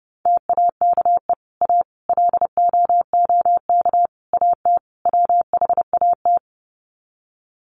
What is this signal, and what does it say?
Morse code, heard as a keyed tone, about 28 wpm, 700 Hz. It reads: TAKE A LOOK AT WHAT
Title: Sets of 5 Words Encore2 30wpm